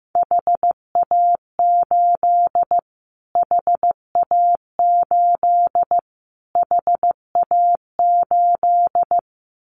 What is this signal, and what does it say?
Morse code, heard as a keyed tone, about 15 wpm, 700 Hz. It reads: HA8 HA8 HA8
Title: Callsign Prefixes for Top 25 DXCC - Repeated 3x - 15wpm